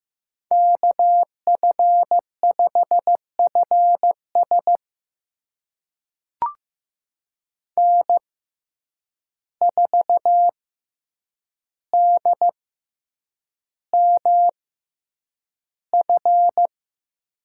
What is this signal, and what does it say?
Morse code, heard as a keyed tone, about 15 wpm, spaced as for 8 wpm, 700 Hz. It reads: KF5FS N4DMF
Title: USA Call Signs - 5 Characters - 6x Character Spacing - 15wpm